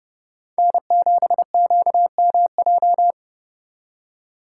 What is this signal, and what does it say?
Morse code, heard as a keyed tone, about 33 wpm, 700 Hz. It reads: N7QMJ